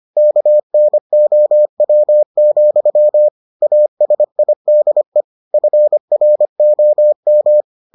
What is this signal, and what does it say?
Morse code, heard as a keyed tone, about 25 wpm, 600 Hz. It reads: KNOW, ASIDE FROM